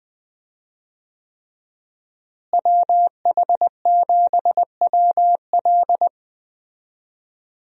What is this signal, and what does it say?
Morse code, heard as a keyed tone, about 20 wpm, 700 Hz. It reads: WH7WL
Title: USA Call Signs - 5 Characters - Repeated 3x - 20wpm